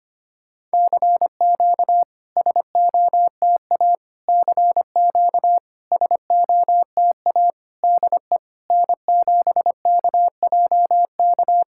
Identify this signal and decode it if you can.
Morse code, heard as a keyed tone, about 25 wpm, 690 Hz. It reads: CQ SOTA CQ SOTA DE N7KJK
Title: SOTA QSO Practice 25wpm